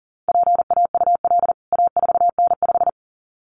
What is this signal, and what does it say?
Morse code, heard as a keyed tone, about 40 wpm, 700 Hz. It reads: PAUL A4N5
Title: CWT Exchange 40wpm